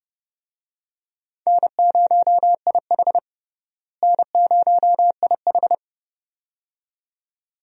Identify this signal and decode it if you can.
Morse code, heard as a keyed tone, about 30 wpm, 700 Hz. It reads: N0IH N0IH
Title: USA Call Signs - 4 Characters - Repeated 2x - 30wpm